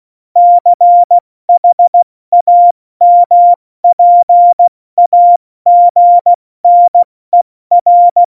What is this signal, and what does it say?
Morse code, heard as a keyed tone, about 16 wpm, 700 Hz. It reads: CHAMPAGNER
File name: Morsekurs / Witze / Witz 17. November 2020, 70 F80